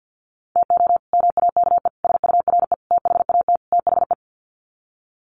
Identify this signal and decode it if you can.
Morse code, heard as a keyed tone, about 50 wpm, 700 Hz. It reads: TO MAKE SURE THAT THE
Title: Sets of 5 Words 50wpm